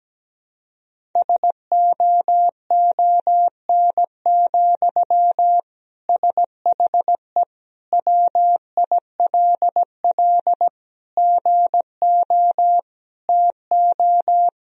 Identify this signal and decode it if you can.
Morse code, heard as a keyed tone, about 17 wpm, 700 Hz. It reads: SOON, SHE WILL GO TO
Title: Sentences from Top 1000 Words - Review Part 1 - 17wpm